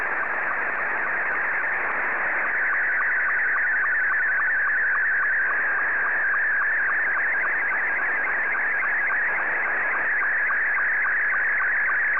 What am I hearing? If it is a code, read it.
SSTV (slow-scan television), Scottie 2.